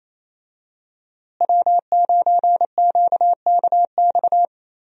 Morse code, heard as a keyed tone, about 27 wpm, 700 Hz. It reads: W9QKX
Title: USA Call Signs - 5 Characters - Repeated 2x - 28wpm